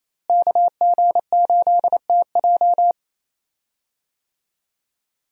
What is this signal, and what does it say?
Morse code, heard as a keyed tone, about 28 wpm, 700 Hz. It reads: KG8TJ
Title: USA Call Signs - 5 Characters - Repeated 2x - 28wpm